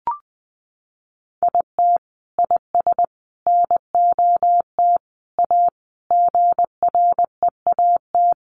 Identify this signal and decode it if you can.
Morse code, heard as a keyed tone, about 20 wpm, 700 Hz. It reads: IT IS NOT A GREAT
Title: Sentences from Top 800 Words 20wpm